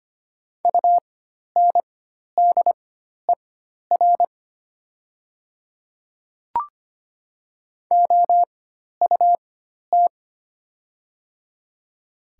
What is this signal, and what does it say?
Morse code, heard as a keyed tone, about 25 wpm, spaced as for 12 wpm, 700 Hz. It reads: UNDER OUT
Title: 4x Character Spacing - Top 200 Words - 25wpm